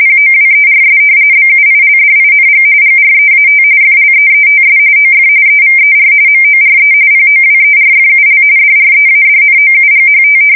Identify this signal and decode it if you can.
RTTY, about 45.45 baud. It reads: POWERS BUILDING GREAT FLEETS OF EVER MORE POWERFUL DREADNOUGHT
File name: fsk170 / battleshipnj